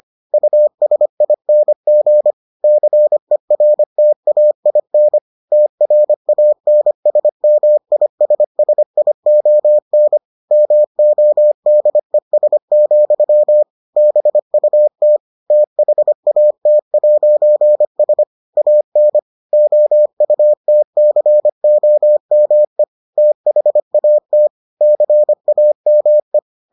Morse code, heard as a keyed tone, about 25 wpm, 600 Hz. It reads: USING CERTAIN TRANSMISSION MODES, BUT THAT'S AN OUTCOME THAT CAME